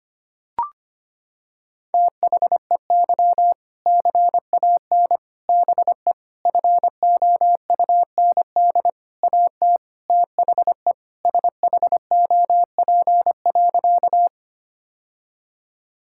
Morse code, heard as a keyed tone, about 25 wpm, 700 Hz. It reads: THEY CAN BE FOUND AT THE SHOP.
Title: Sentences from Top 1600 Words 25wpm